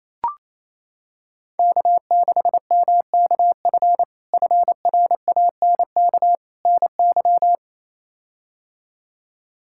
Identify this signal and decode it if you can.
Morse code, heard as a keyed tone, about 28 wpm, 700 Hz. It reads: K6MKF FRANK NY
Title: K1USN SST QSO Practice - 28wpm